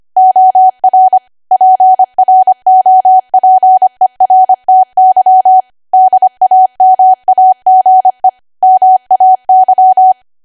Morse code, heard as a keyed tone, about 25 wpm, 750 Hz. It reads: OR PROPERTY DAMAGE MAY
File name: cwa / W1AW-Level3 / 25WPM